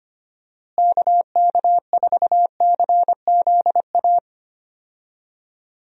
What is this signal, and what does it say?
Morse code, heard as a keyed tone, about 25 wpm, 700 Hz. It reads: KK4CZA
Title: USA Call Signs - 6 Characters - Repeated 3x - 25wpm